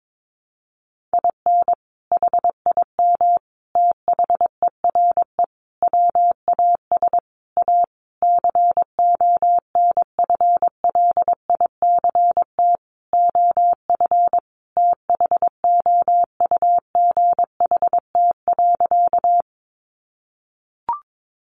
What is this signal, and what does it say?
Morse code, heard as a keyed tone, about 22 wpm, 700 Hz. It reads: IN HIM THERE WAS A CONFLICT OF THOUGHT.
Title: Sentences from Top 1300 Words 22wpm